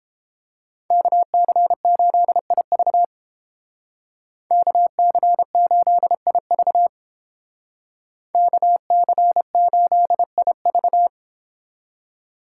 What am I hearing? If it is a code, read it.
Morse code, heard as a keyed tone, about 28 wpm, 700 Hz. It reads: KC8IV KC8IV KC8IV